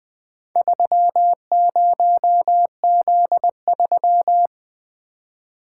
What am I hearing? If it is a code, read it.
Morse code, heard as a keyed tone, about 20 wpm, 700 Hz. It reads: 30Z3
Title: Four Letter-Number Combinations - 20wpm